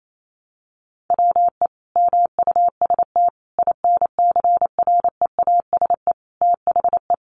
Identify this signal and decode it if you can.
Morse code, heard as a keyed tone, about 28 wpm, 700 Hz. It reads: WE MUST INCREASE THE
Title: Sentences from Top 1300 Words 28wpm